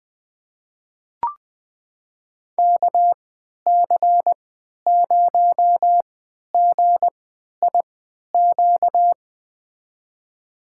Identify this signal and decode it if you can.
Morse code, heard as a keyed tone, about 20 wpm, spaced as for 11 wpm, 700 Hz. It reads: KC0GIQ
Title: USA Call Signs - 6 Characters - 3x Character Spacing - 20wpm